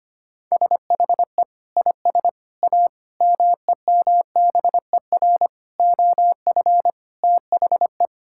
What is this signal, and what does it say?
Morse code, heard as a keyed tone, about 25 wpm, 700 Hz. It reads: SHE IS A MEMBER OF THE
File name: practice / 25wpm / Sentences with 4-Word Phrases - 25wpm